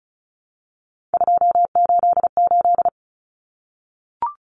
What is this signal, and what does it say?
Morse code, heard as a keyed tone, about 34 wpm, 700 Hz. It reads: 288
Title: Common Cut Numbers - 3-Digit 35wpm